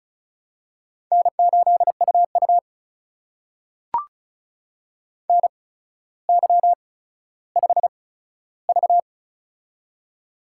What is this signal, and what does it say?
Morse code, heard as a keyed tone, about 36 wpm, spaced as for 15 wpm, 700 Hz. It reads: N8UU NY5V
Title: USA Call Signs - 4 Characters - 8x Character Spacing - 35wpm